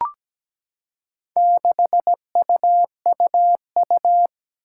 Morse code, heard as a keyed tone, about 17 wpm, 700 Hz. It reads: 6UUU
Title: Four Challenging Letter-Number Combinations 17wpm